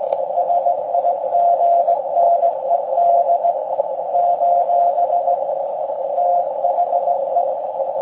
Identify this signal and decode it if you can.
Morse code, heard as a keyed tone, about 18 wpm, 690 Hz, with noise. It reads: SIGNED : THE
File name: forum / im / aq grimeton